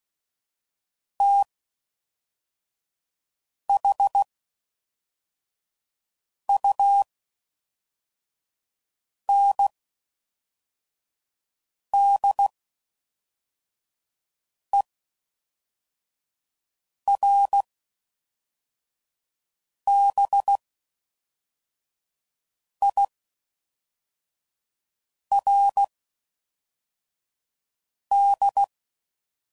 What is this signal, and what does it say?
Morse code, heard as a keyed tone, about 16 wpm, spaced as for 4 wpm, 800 Hz. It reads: THUNDERBIRD